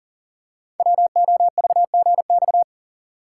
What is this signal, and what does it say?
Morse code, heard as a keyed tone, about 41 wpm, 700 Hz. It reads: WOVGX
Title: Code Groups - 5 Letters - 40wpm